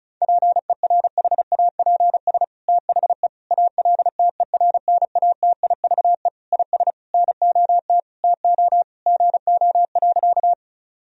Morse code, heard as a keyed tone, about 36 wpm, 700 Hz. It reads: PERHAPS THE ALTERNATIVE IS NOT TO GO.
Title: Sentences from Top 1000 Words - Review Part 4 - 35wpm